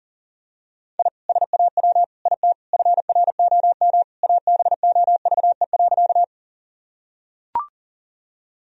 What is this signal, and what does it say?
Morse code, heard as a keyed tone, about 40 wpm, 700 Hz. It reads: I SAW IT FROM ABOVE.
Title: Sentences from Top 1000 Words - Review Part 2 - 40wpm